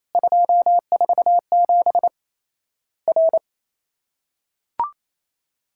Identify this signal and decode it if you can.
Morse code, heard as a keyed tone, about 28 wpm, 700 Hz. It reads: 247 R
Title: Worked All Europe DX Contest - QTC Exchange - 28wpm